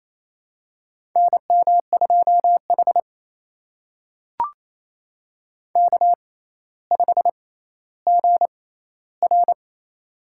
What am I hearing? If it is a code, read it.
Morse code, heard as a keyed tone, about 27 wpm, spaced as for 14 wpm, 700 Hz. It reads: NM2H K5GR
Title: USA Call Signs - 4 Characters - 6x Character Spacing - 28wpm